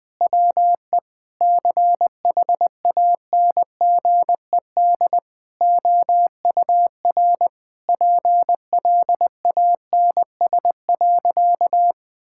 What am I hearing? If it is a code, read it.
Morse code, heard as a keyed tone, about 20 wpm, 700 Hz. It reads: WE CHANGED OUR PLANS.
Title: Sentences with 5-Word Phrases - 20wpm